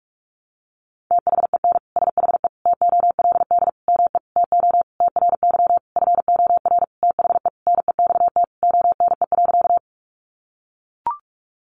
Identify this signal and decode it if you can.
Morse code, heard as a keyed tone, about 44 wpm, 700 Hz. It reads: THEN SHE TOLD ME TO TRY FOR THE NEXT ONE.
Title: Sentences from Top 400 Words 45wpm